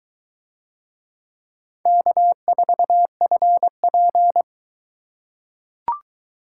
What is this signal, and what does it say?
Morse code, heard as a keyed tone, about 20 wpm, 700 Hz. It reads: K4FP